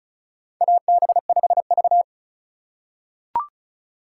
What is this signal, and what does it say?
Morse code, heard as a keyed tone, about 36 wpm, 700 Hz. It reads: AB5V
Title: USA Call Signs - 4 Characters - 4x Character Spacing - 35wpm